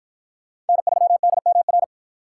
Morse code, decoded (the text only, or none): N3DMR